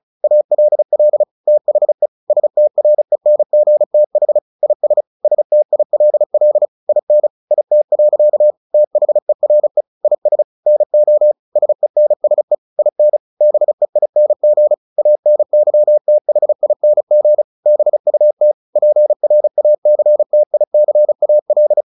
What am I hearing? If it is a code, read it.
Morse code, heard as a keyed tone, about 34 wpm, 600 Hz. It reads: ALL THE STRENGTH IS STILL IN IT. THERE IS NO SENSE IN BEING ANYTHING BUT PRACTICAL
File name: the old man and the sea / the old man and the sea 1 35wpm